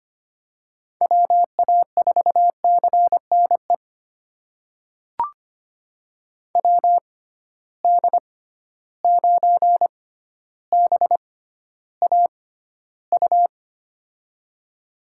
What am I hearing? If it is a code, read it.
Morse code, heard as a keyed tone, about 25 wpm, spaced as for 13 wpm, 700 Hz. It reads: WA4CNE WD9BAU
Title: USA Call Signs - 6 Characters - 6x Character Spacing - 25wpm